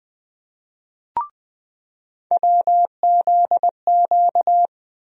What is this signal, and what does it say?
Morse code, heard as a keyed tone, about 20 wpm, 700 Hz. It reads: WZQ
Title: USA Call Sign Postfixes - 3 Letters 20wpm